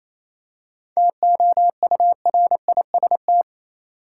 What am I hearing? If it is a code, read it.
Morse code, heard as a keyed tone, about 28 wpm, 700 Hz. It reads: TOURIST
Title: Seven Letter Words - 28wpm